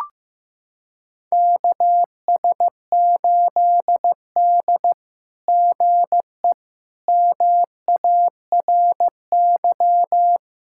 Morse code, heard as a keyed tone, about 15 wpm, 700 Hz. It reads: KS8D GE MARY
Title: SOTA QSO Practice 15wpm